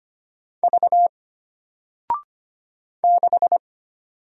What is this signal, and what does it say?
Morse code, heard as a keyed tone, about 25 wpm, 700 Hz. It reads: V 6
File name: Single-Letter-Number-25wpm